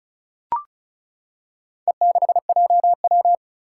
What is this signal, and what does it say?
Morse code, heard as a keyed tone, about 36 wpm, 700 Hz. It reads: E6JW
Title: Four Challenging Letter-Number Combinations 35wpm